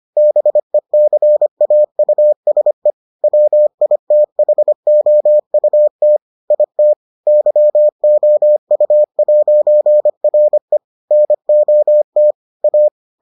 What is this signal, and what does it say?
Morse code, heard as a keyed tone, about 25 wpm, 600 Hz. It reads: BECAUSE WITHOUT IT YOU'RE NOT A